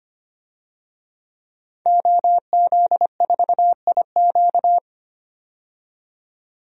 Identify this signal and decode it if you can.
Morse code, heard as a keyed tone, about 25 wpm, 690 Hz. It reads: OZ4IQ
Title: DXCC-100 Call Signs - 5 Characters - 25wpm